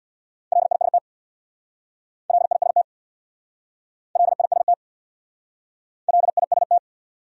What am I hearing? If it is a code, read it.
Morse code, heard as a keyed tone, about 64 wpm, 700 Hz. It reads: LIST LIST LIST LIST